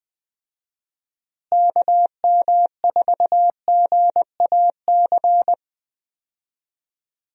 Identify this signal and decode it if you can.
Morse code, heard as a keyed tone, about 20 wpm, 700 Hz. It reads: KM4GAC
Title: USA Call Signs - 6 Characters - Repeated 3x - 20wpm